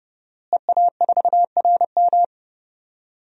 Morse code, heard as a keyed tone, about 31 wpm, 700 Hz. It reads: EA4RM
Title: DXCC-100 Call Signs - 5 Characters - Repeated 3x - 30wpm